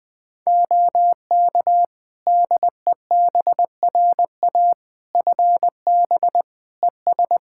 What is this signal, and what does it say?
Morse code, heard as a keyed tone, about 20 wpm, 700 Hz. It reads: OK DEBRA FB ES
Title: Practice QSO - Exchange 2 - Info - 20wpm